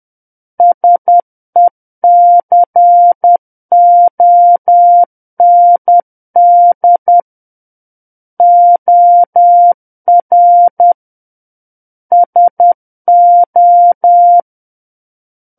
Morse code, heard as a keyed tone, about 10 wpm, 700 Hz. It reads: SECOND OR SO
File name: Book One / 01-Chapter 1 - 10 WPM